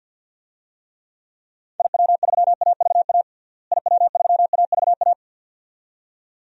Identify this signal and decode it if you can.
Morse code, heard as a keyed tone, about 47 wpm, 700 Hz. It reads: IW3AVA IW3AVA
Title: DXCC-100 Call Signs - 6 Characters - Repeated 2x - 50wpm